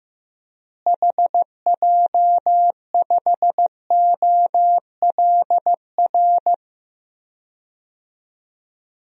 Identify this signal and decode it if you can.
Morse code, heard as a keyed tone, about 15 wpm, 700 Hz. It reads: HJ5OLR
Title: DXCC-100 Call Signs - 6 Characters - Repeated 3x - 15wpm